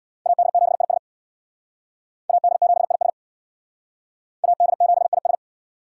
Morse code, heard as a keyed tone, about 59 wpm, 700 Hz. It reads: AD7IS AD7IS AD7IS
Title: Speed Racing - USA 2x2 Call Signs 50wpm